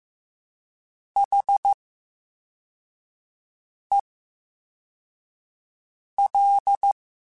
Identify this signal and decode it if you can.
Morse code, heard as a keyed tone, about 15 wpm, 800 Hz. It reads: H E L
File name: Special INGs-mp3s / Special INGs-mp3s / VerySlowToFaster Ing @13